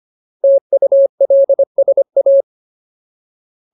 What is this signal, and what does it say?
Morse code, heard as a keyed tone, about 25 wpm, 550 Hz. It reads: TULSA
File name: Ultimate Word List / 25WPM Ultimate